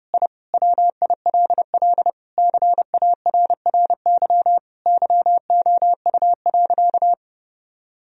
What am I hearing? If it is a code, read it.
Morse code, heard as a keyed tone, about 30 wpm, 700 Hz. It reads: I WILL CARRY YOU.
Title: Sentences from Top 1000 Words - Review Part 1 - 30wpm